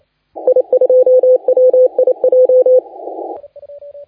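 Morse code, heard as a keyed tone, about 30 wpm, 510 Hz. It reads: I2WIJ
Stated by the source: Morse contest audio recorded by LA4C during the SAC CW 2018 contest.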